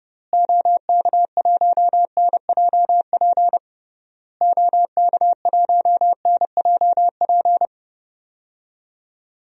Morse code, heard as a keyed tone, about 30 wpm, 700 Hz. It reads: OK1NJP OK1NJP
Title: DXCC-100 Call Signs - 6 Characters - Repeated 3x - 30wpm